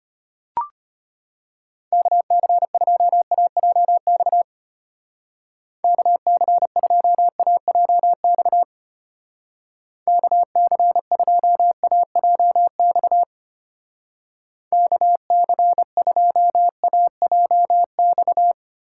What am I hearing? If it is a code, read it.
Morse code, heard as a keyed tone, about 32 wpm, 700 Hz. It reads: KC2AJX KC2AJX KC2AJX KC2AJX